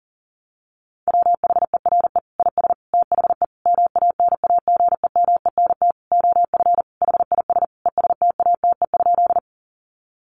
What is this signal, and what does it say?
Morse code, heard as a keyed tone, about 40 wpm, 700 Hz. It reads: WHERE IS THE MANAGEMENT OF HIS ESTATE?
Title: Sentences from Top 1200 Words 40wpm